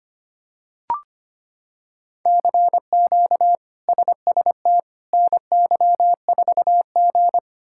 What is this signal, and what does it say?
Morse code, heard as a keyed tone, about 25 wpm, 700 Hz. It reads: CQ SST NY4G
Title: K1USN SST QSO Practice - 25wpm